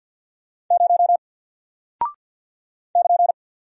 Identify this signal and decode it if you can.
Morse code, heard as a keyed tone, about 49 wpm, 700 Hz. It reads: 0 <KN>
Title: Single Character (Letter:Number:Punctuation:Prosign) Alt 50wpm